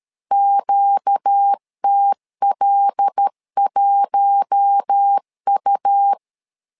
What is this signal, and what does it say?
Morse code, heard as a keyed tone, about 13 wpm, 800 Hz. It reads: QTL1U
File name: LEZIONI / 5 LBVC / 5 L 12wpm 1913